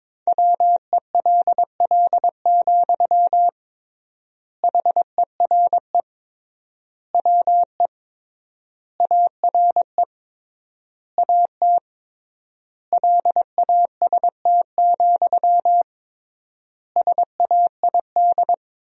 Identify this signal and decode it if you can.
Morse code, heard as a keyed tone, about 21 wpm, 700 Hz. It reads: WELL, HERE WE ARE AT LAST, SAID
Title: 3x Word-Spacing - Sentences from Top 300 Words - 22wpm